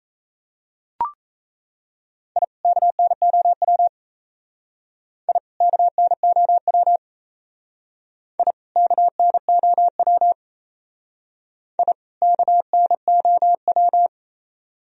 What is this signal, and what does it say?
Morse code, heard as a keyed tone, about 34 wpm, 700 Hz. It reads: I KNOW I KNOW I KNOW I KNOW
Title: Speed-Racing - Sets of 2 Words 28wpm